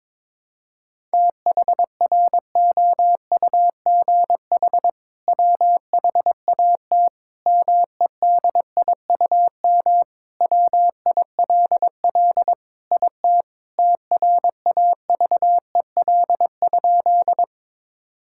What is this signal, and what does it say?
Morse code, heard as a keyed tone, about 21 wpm, 700 Hz. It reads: THROUGH WHAT MEDIUM WILL IT TRAVEL?
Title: Sentences from Top 1000 Words - Review Part 6 - 22wpm